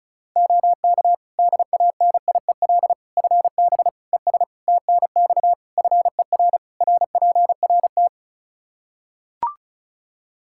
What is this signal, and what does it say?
Morse code, heard as a keyed tone, about 36 wpm, 700 Hz. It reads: OK DANIEL FB ES TNX FER RPRT
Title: K1USN SST QSO Practice 35wpm